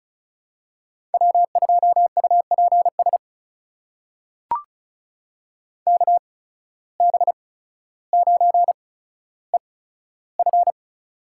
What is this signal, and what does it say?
Morse code, heard as a keyed tone, about 36 wpm, spaced as for 15 wpm, 700 Hz. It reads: W2UPS KB9EF
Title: USA Call Signs - 5 Characters - 8x Character Spacing - 35wpm